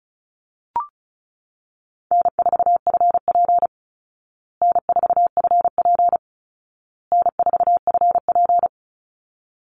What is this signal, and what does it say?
Morse code, heard as a keyed tone, about 34 wpm, 700 Hz. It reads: N4FP N4FP N4FP
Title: Call Signs Repeated 2x - 35wpm